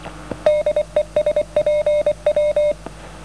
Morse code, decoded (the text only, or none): DE SPW